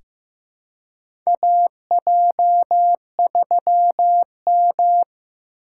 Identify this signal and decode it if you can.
Morse code, heard as a keyed tone, about 15 wpm, 700 Hz. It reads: AJ3M
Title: USA Call Signs - 4 Characters - 3x Character Spacing - 15wpm